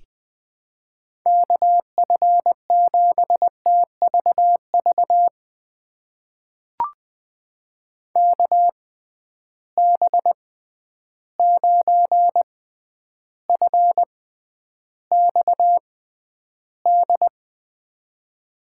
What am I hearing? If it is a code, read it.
Morse code, heard as a keyed tone, about 20 wpm, spaced as for 10 wpm, 700 Hz. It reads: KF7TVV KB9FXD
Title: USA Call Signs - 6 Characters - 6x Character Spacing - 20wpm